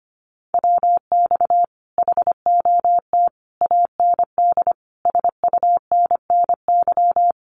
Morse code, heard as a keyed tone, about 25 wpm, 700 Hz. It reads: WX HOT AND SUNNY
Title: Common QSO Phrases Encore 25wpm